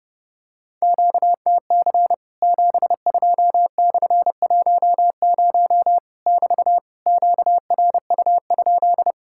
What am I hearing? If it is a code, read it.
Morse code, heard as a keyed tone, about 30 wpm, 700 Hz. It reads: QTC 72/10 <BT> QRU?
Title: Worked All Europe DX Contest - QTC Exchange - 30wpm